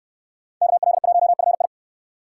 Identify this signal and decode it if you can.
Morse code, heard as a keyed tone, about 60 wpm, 700 Hz. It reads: KL1FI